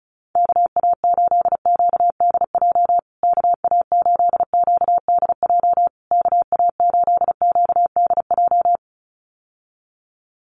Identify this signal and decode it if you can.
Morse code, heard as a keyed tone, about 36 wpm, 700 Hz. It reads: KA8QDJ KA8QDJ KA8QDJ
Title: USA Call Signs Repeated 3x 35wpm